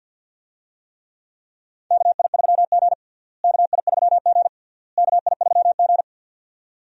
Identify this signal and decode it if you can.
Morse code, heard as a keyed tone, about 47 wpm, 700 Hz. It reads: KI3G KI3G KI3G
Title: USA Call Signs - 4 Characters - Repeated 3x - 50wpm